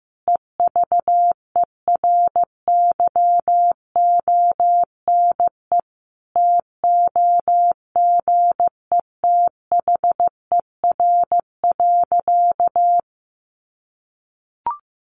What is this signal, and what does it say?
Morse code, heard as a keyed tone, about 15 wpm, 700 Hz. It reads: EVERYONE TOGETHER.
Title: Sentences from Top 1500 Words 15wpm